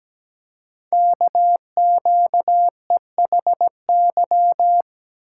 Morse code, heard as a keyed tone, about 17 wpm, 700 Hz. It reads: KQEHY
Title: Code Groups - 5 Letters - 17wpm